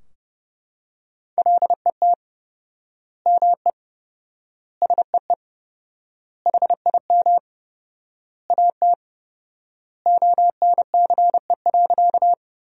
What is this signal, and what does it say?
Morse code, heard as a keyed tone, about 30 wpm, 700 Hz. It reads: LET ME SEE HIM AT ONCE.
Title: 4x Word-Spacing - Sentences from Top 300 Words - 30wpm